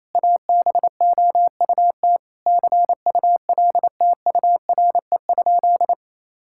Morse code, heard as a keyed tone, about 28 wpm, 700 Hz. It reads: ABOUT CULTURE?
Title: Sentences from Top 1000 Words - Review Part 6 - 28wpm